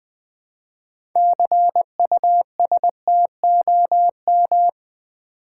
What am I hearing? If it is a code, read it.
Morse code, heard as a keyed tone, about 20 wpm, 700 Hz. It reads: CUSTOM
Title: Build A Word - 2 Letters At A Time - 8-Letter Words - 20wpm